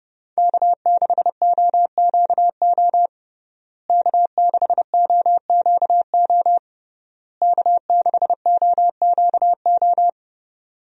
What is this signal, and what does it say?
Morse code, heard as a keyed tone, about 30 wpm, 700 Hz. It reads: K6OQO K6OQO K6OQO
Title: USA Call Signs - 5 Characters - Repeated 3x - 30wpm